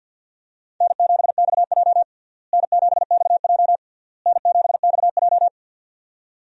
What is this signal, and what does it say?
Morse code, heard as a keyed tone, about 46 wpm, 700 Hz. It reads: N7XJ N7XJ N7XJ
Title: USA Call Signs - 4 Characters - Repeated 3x - 50wpm